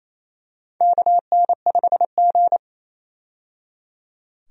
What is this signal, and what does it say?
Morse code, heard as a keyed tone, about 28 wpm, 700 Hz. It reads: KN5G
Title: USA Call Signs - 4 Characters - 28wpm